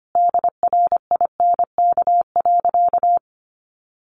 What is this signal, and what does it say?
Morse code, heard as a keyed tone, about 25 wpm, 700 Hz. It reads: DRINK.
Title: Sentences from Top 1500 Words 25wpm